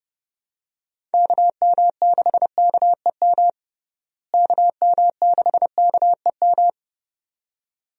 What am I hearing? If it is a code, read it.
Morse code, heard as a keyed tone, about 31 wpm, 700 Hz. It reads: KM6KEM KM6KEM
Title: USA Call Signs - 6 Characters - Repeated 2x - 30wpm